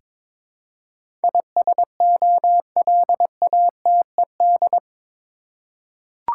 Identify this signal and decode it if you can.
Morse code, heard as a keyed tone, about 22 wpm, 700 Hz. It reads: ISOLATED
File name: practice / 22wpm / Eight Letter Words - 22wpm